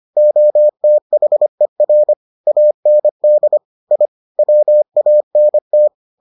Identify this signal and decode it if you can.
Morse code, heard as a keyed tone, about 25 wpm, 600 Hz. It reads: OTHER AND I WANT